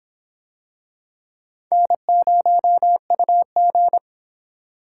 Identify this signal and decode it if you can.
Morse code, heard as a keyed tone, about 26 wpm, 700 Hz. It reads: N0UG